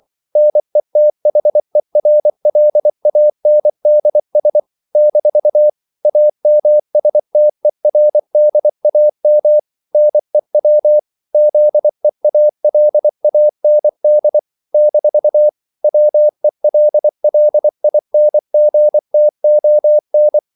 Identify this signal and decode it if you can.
Morse code, heard as a keyed tone, about 24 wpm, 600 Hz. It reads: NETHERLANDS - AMSTERDAM NEW ZEALAND - WELLINGTON
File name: CW / Practice / NationalCapitals24